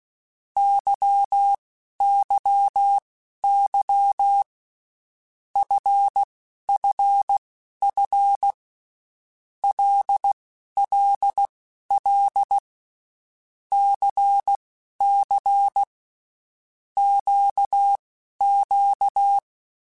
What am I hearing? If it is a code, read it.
Morse code, heard as a keyed tone, about 16 wpm, spaced as for 11 wpm, 800 Hz. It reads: YYY FFF LLL CC QQ
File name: Groups@10 / Groups@10 mp3s / LFCYQ@10